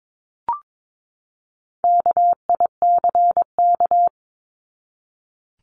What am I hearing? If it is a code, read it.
Morse code, heard as a keyed tone, about 22 wpm, 700 Hz. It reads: KICK
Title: Top 1251-1500 Words 22wpm